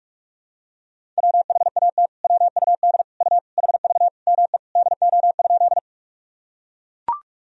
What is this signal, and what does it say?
Morse code, heard as a keyed tone, about 41 wpm, 700 Hz. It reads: WHAT WUD U HV ME DO?
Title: Sentences from Top 200 Words and Abbreviations 45wpm